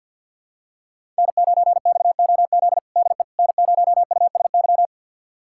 Morse code, heard as a keyed tone, about 53 wpm, 700 Hz. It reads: N9XOZ DE N0USY
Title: USA Callsign DE Callsign - 5 Characters - 50wpm